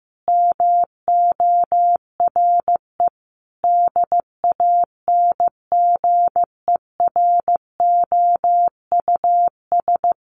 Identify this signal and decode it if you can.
Morse code, heard as a keyed tone, about 15 wpm, 700 Hz. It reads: MORE DANGEROUS
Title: Sentences from Top 1400 Words 15wpm